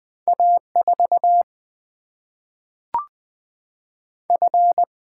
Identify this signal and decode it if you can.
Morse code, heard as a keyed tone, about 20 wpm, 700 Hz. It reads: A4 F
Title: Single-Letter Single-Number 20wpm